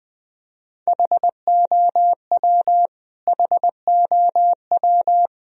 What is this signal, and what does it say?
Morse code, heard as a keyed tone, about 20 wpm, 700 Hz. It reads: HOW HOW
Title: Top 100 Words Repeated 20wpm